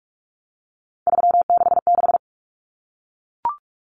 Morse code, heard as a keyed tone, about 44 wpm, 700 Hz. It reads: 366
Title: Numbers - 3-Digit 45wpm